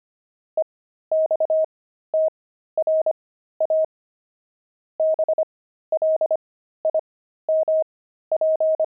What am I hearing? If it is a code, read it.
Morse code, heard as a keyed tone, about 25 wpm, spaced as for 13 wpm, 630 Hz. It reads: EXTRA BLIMP
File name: pangram-fs-13-cs-25-blimp